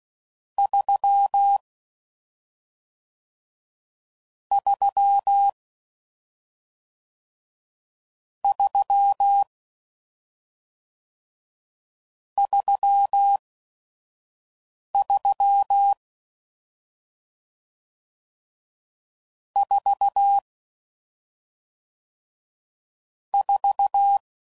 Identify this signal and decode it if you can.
Morse code, heard as a keyed tone, about 16 wpm, spaced as for 5 wpm, 800 Hz. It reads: 3 3 3 33 4 4